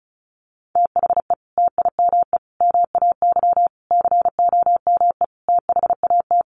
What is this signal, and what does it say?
Morse code, heard as a keyed tone, about 34 wpm, 700 Hz. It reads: THE TIME MAY COME THAT
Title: Sentences from Top 1100 Words 35wpm